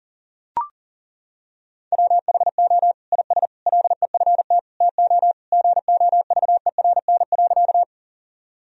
Morse code, heard as a keyed tone, about 40 wpm, 700 Hz. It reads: WHO IS LEFT TO GOVERN.
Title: Sentences from Top 1000 Words - Review Part 2 - 40wpm